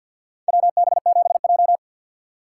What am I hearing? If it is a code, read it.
Morse code, heard as a keyed tone, about 51 wpm, 700 Hz. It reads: WB7J
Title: USA Call Signs - 4 Characters - 2x Character Spacing - 50wpm